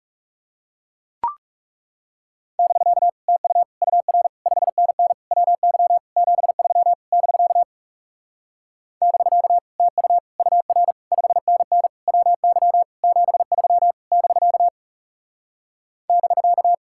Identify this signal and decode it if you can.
Morse code, heard as a keyed tone, about 37 wpm, 700 Hz. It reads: <BK> TU UR 5NN WY 73 <BK> <BK> TU UR 5NN WY 73 <BK> <BK>
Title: Speed-Racing Common QSO Phrases - 30wpm